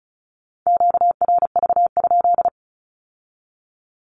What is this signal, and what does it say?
Morse code, heard as a keyed tone, about 36 wpm, 700 Hz. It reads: QRV?
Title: Q-Codes with Spoken Meaning - 35wpm